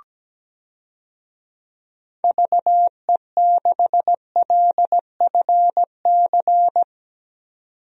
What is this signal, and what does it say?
Morse code, heard as a keyed tone, about 17 wpm, 700 Hz. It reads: VE6LFC
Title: DXCC-100 Call Signs - 6 Characters - 17wpm